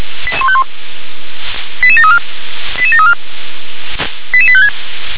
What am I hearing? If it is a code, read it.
Five-tone selective call (ZVEI), heard as quick runs of five tones, tones of about 70 ms. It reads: 88151 88734 88734 88856